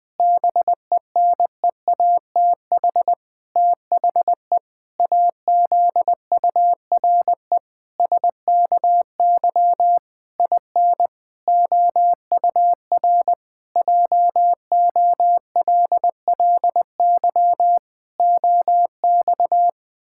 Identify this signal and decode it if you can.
Morse code, heard as a keyed tone, about 19 wpm, 700 Hz. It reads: BENEATH THE AZURE SKY IN OUR JOLLY OX
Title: Pangram Challenge v2 - 20wpm